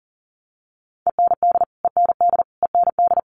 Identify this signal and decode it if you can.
Morse code, heard as a keyed tone, about 40 wpm, 700 Hz. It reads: END END END
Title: Top 500 Words Repeated 40wpm